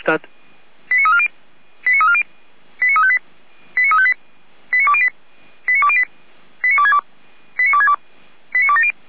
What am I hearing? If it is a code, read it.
Five-tone selective call (ZVEI), heard as quick runs of five tones, tones of about 70 ms, with speech. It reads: 89340 89340 89358 89358 89208 89208 89262 89262 89270